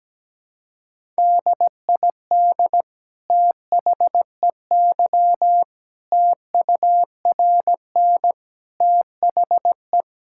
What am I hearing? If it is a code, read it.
Morse code, heard as a keyed tone, about 17 wpm, 700 Hz. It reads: DID THEY TURN THE
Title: Sentences from Top 1000 Words - Review Part 5 - 17wpm